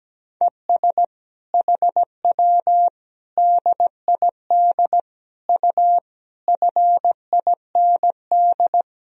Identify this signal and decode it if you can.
Morse code, heard as a keyed tone, about 17 wpm, 700 Hz. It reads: ES HW DID U FIND
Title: Sentences from Top 200 Words and Abbreviations 17wpm